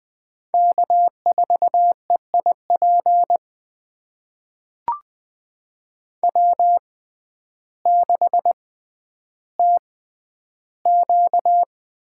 Morse code, heard as a keyed tone, about 21 wpm, spaced as for 11 wpm, 700 Hz. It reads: K4EIP W6TQ
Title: USA Call Signs - 5 Characters - 6x Character Spacing - 20wpm